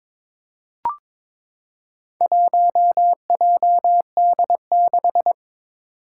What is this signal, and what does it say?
Morse code, heard as a keyed tone, about 22 wpm, 700 Hz. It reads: 1JD6
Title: Four Challenging Letter-Number Combinations 22wpm